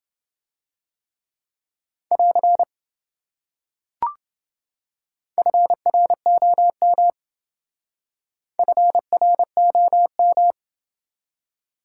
Morse code, heard as a keyed tone, about 28 wpm, 700 Hz. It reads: <AR> FROM FROM